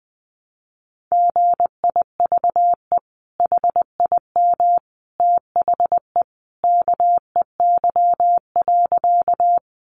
Morse code, heard as a keyed tone, about 20 wpm, 700 Hz. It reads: GIVE HIM THE KEY.
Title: Sentences from Top 600 Words 20wpm